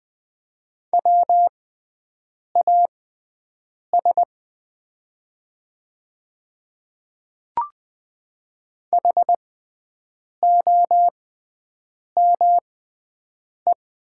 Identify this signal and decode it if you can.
Morse code, heard as a keyed tone, about 20 wpm, spaced as for 7 wpm, 700 Hz. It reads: WAS HOME
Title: 6x Character Spacing - Top 300 Words - 20wpm